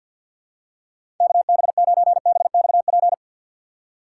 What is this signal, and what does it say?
Morse code, heard as a keyed tone, about 47 wpm, 700 Hz. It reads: KB9BXP
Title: USA Call Signs - 6 Characters - 4x Character Spacing - 50wpm